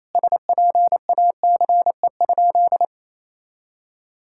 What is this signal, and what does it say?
Morse code, heard as a keyed tone, about 28 wpm, 700 Hz. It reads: SPACE?
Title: Sentences from Top 1000 Words - Review Part 5 - 28wpm